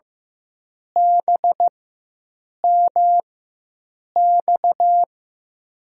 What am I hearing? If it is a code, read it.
Morse code, heard as a keyed tone, about 15 wpm, spaced as for 7 wpm, 700 Hz. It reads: BMX
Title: USA Call Signs - 5 Characters - 4x Character Spacing - 15wpm